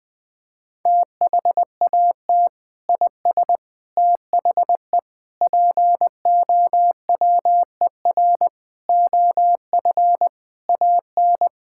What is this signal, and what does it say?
Morse code, heard as a keyed tone, about 20 wpm, 700 Hz. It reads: THAT IS THE POWER OF AN
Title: Sentences from Top 1000 Words - Review Part 5 - 20wpm